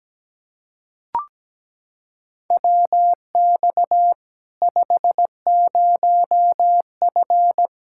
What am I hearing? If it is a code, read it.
Morse code, heard as a keyed tone, about 17 wpm, 700 Hz. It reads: WX 50F
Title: K1USN SST QSO Practice 17wpm